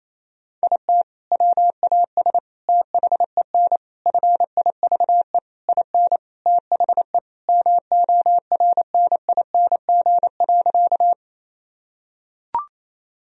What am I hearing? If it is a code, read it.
Morse code, heard as a keyed tone, about 27 wpm, 700 Hz. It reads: IT WAS THEN FIVE IN THE MORNING.
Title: Sentences from Top 1000 Words - Review Part 2 - 28wpm